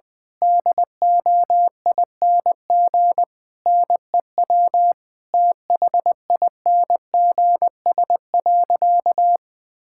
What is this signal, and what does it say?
Morse code, heard as a keyed tone, about 20 wpm, 700 Hz. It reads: DOING NEW THINGS.
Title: Difficult Sentences with Hints - 20wpm